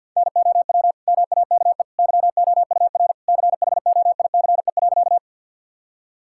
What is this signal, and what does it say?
Morse code, heard as a keyed tone, about 50 wpm, 700 Hz. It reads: NOW MAKE YOUR CHOICE.
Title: Sentences from Top 1000 Words - Review Part 6 - 50wpm